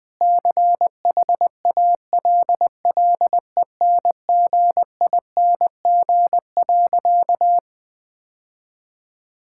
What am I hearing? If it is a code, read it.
Morse code, heard as a keyed tone, about 20 wpm, 700 Hz. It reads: CHALLENGING.
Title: Sentences with 5-Word Phrases - 20wpm